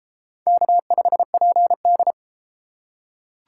Morse code, heard as a keyed tone, about 32 wpm, 700 Hz. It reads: K5PD